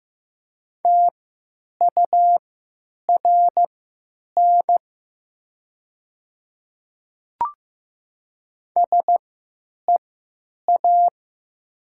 Morse code, heard as a keyed tone, about 15 wpm, spaced as for 9 wpm, 700 Hz. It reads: TURN SEA
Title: 3x Character Spacing - Top 300 Words - 15wpm